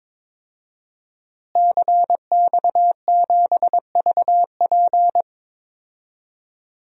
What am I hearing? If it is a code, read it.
Morse code, heard as a keyed tone, about 22 wpm, 700 Hz. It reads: CX7VP
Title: DXCC-100 Call Signs - 5 Characters - Repeated 2x - 22wpm